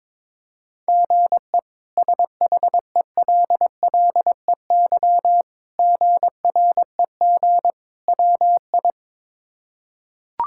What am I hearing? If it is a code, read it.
Morse code, heard as a keyed tone, about 22 wpm, 700 Hz. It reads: GE SHELLEY GREG WI
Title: K1USN SST QSO Practice - 22wpm